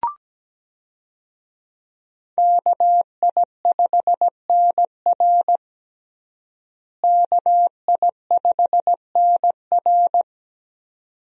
Morse code, heard as a keyed tone, about 17 wpm, 700 Hz. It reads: KI5NR KI5NR
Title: USA Call Signs - 5 Characters - Repeated 2x - 17wpm